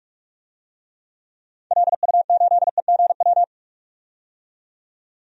Morse code, heard as a keyed tone, about 46 wpm, 700 Hz. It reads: LU8EGW
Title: DXCC-100 Call Signs - 6 Characters - 45wpm